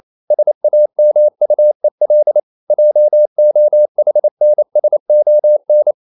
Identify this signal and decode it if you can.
Morse code, heard as a keyed tone, about 28 wpm, 600 Hz. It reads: SAMUEL JOHNSON
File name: CW / Practice / EnglishQuotations0228